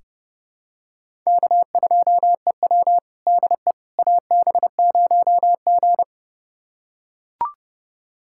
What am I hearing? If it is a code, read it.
Morse code, heard as a keyed tone, about 32 wpm, 700 Hz. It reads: K2EW DE AB0G
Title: USA Callsign DE Callsign - 4 Characters - 30wpm